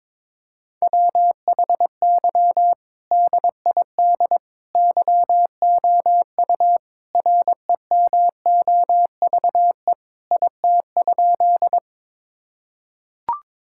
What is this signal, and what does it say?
Morse code, heard as a keyed tone, about 22 wpm, 700 Hz. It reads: WHY DID YOU REMOVE IT?
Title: Sentences from Top 1000 Words - Review Part 5 - 22wpm